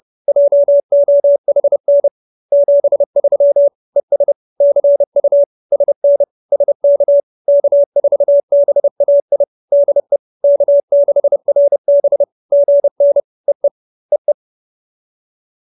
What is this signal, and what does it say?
Morse code, heard as a keyed tone, about 30 wpm, 570 Hz. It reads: JOHN 73 ES CU SN SK K4BAI DE K6RB GN EE EE